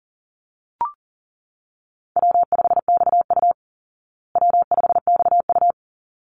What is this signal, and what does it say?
Morse code, heard as a keyed tone, about 37 wpm, 700 Hz. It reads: W5XU W5XU
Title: Easy Call Signs Repeated 2x - 40wpm